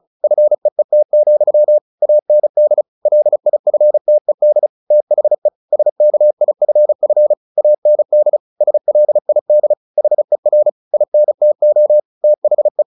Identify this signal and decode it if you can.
Morse code, heard as a keyed tone, about 34 wpm, 600 Hz. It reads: FEET, AND LIFTED THE SKIFF AND SLID HER INTO THE
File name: the old man and the sea / the old man and the sea 1 35wpm